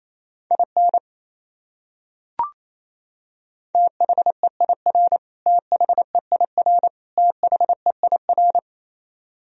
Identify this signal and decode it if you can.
Morse code, heard as a keyed tone, about 28 wpm, 700 Hz. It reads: IN THEIR THEIR THEIR
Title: Top 100 Words Repeated 28wpm